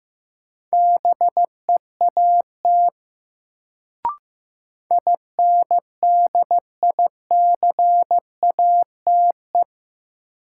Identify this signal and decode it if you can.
Morse code, heard as a keyed tone, about 15 wpm, 700 Hz. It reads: BEAT INDICATE
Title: Sentences from Top 1300 Words - New Word Drill Alt - 15wpm